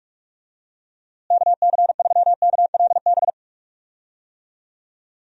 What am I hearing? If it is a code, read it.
Morse code, heard as a keyed tone, about 46 wpm, 700 Hz. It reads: KC3KLB
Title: USA Call Signs - 6 Characters - Repeated 3x - 45wpm